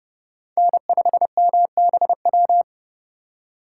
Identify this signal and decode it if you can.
Morse code, heard as a keyed tone, about 30 wpm, 700 Hz. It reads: N5MBW
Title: USA Call Signs - 5 Characters - 2x Character Spacing - 30wpm